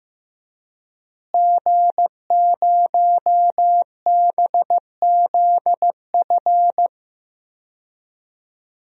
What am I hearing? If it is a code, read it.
Morse code, heard as a keyed tone, about 15 wpm, 700 Hz. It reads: G0BZF
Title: DXCC-100 Call Signs - 5 Characters - Repeated 2x - 15wpm